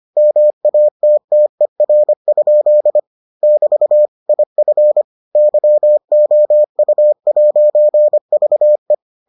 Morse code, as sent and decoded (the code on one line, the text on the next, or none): -- .- - - . .-. ..--.. / -...- / .. ..-. / -.-- --- ..- .----. ...- .
MATTER? <BT> IF YOU'VE